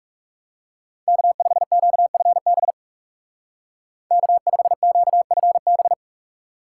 Morse code, heard as a keyed tone, about 35 wpm, 700 Hz. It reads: K5QFB K5QFB
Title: Speed Racing - USA Call Signs 30wpm